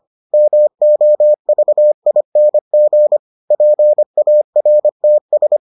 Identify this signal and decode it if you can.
Morse code, heard as a keyed tone, about 25 wpm, 600 Hz. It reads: MOVING PARTS